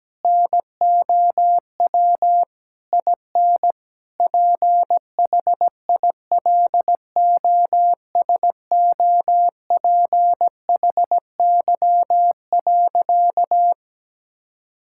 Morse code, heard as a keyed tone, about 17 wpm, 700 Hz. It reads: NOW IN PHILOSOPHY.
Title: Sentences from Top 1000 Words - Review Part 6 - 17wpm